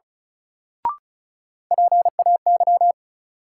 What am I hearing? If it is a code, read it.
Morse code, heard as a keyed tone, about 36 wpm, 700 Hz. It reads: PAY
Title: Sentences from Top 700 Words - New Word Drill - 35wpm